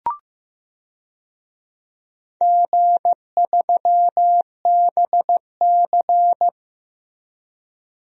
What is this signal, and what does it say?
Morse code, heard as a keyed tone, about 15 wpm, 700 Hz. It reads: G3BC
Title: DXCC-100 Call Signs - 4 Characters - Repeated 2x - 15wpm